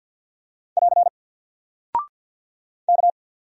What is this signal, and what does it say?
Morse code, heard as a keyed tone, about 50 wpm, 700 Hz. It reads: <AR> K
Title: Single Character (Letter:Number:Punctuation:Prosign) 50wpm